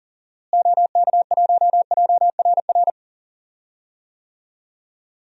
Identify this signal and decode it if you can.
Morse code, heard as a keyed tone, about 39 wpm, 700 Hz. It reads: OK1JRR
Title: DXCC-100 Call Signs - 6 Characters - Repeated 3x - 40wpm